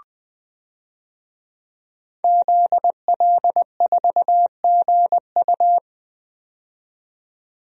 Morse code, heard as a keyed tone, about 20 wpm, 700 Hz. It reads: ZL4GU
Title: DXCC-100 Call Signs - 5 Characters - 20wpm